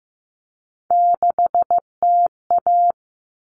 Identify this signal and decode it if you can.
Morse code, heard as a keyed tone, about 15 wpm, 700 Hz. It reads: 6TA
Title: All Cut Numbers - 3-Digit 15wpm